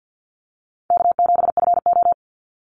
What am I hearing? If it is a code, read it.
Morse code, heard as a keyed tone, about 50 wpm, 700 Hz. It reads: K7FO
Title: NA SOTA Chaser Callsigns 50wpm